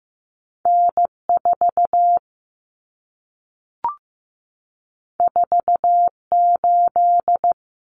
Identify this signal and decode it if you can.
Morse code, heard as a keyed tone, about 15 wpm, 700 Hz. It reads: N4 48
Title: Common Cut Numbers - 2-Digit 15wpm